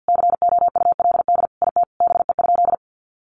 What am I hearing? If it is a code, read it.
Morse code, heard as a keyed tone, about 50 wpm, 700 Hz. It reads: COULD IT BE?
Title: Sentences from Top 100 Words 50wpm